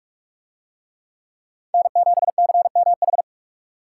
Morse code, heard as a keyed tone, about 40 wpm, 700 Hz. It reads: N7CMH